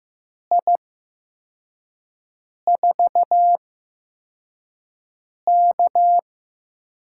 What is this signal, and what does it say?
Morse code, heard as a keyed tone, about 15 wpm, spaced as for 4 wpm, 700 Hz. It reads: I4K
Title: USA Call Signs - 8x Character Spacing - 15wpm